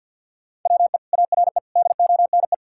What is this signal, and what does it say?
Morse code, heard as a keyed tone, about 50 wpm, 700 Hz. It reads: WE ARE DONE
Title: Sentences from Top 1600 Words 50wpm